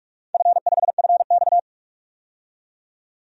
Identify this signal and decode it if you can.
Morse code, heard as a keyed tone, about 47 wpm, 700 Hz. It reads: F5FX
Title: DXCC-100 Call Signs - 4 Characters - 45wpm